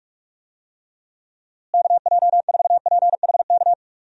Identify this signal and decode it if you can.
Morse code, heard as a keyed tone, about 46 wpm, 700 Hz. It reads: KJ4PHK
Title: USA Call Signs - 6 Characters - Repeated 3x - 45wpm